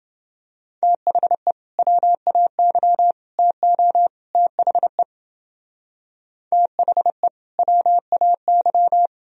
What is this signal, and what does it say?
Morse code, heard as a keyed tone, about 28 wpm, 700 Hz. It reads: THE WAY TO THE THE WAY